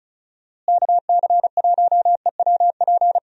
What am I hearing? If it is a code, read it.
Morse code, heard as a keyed tone, about 34 wpm, 700 Hz. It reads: KC1EWP
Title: USA Call Signs - 6 Characters - Repeated 2x - 35wpm